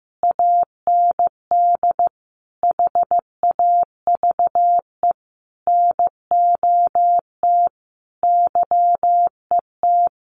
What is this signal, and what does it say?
Morse code, heard as a keyed tone, about 15 wpm, 700 Hz. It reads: AND HAVE NOT YET
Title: Sentences from Top 700 Words 15wpm